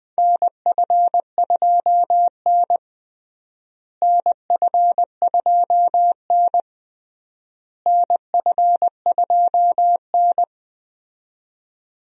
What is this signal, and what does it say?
Morse code, heard as a keyed tone, about 20 wpm, 700 Hz. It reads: NF2N NF2N NF2N
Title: USA Call Signs - 4 Characters - Repeated 3x - 20wpm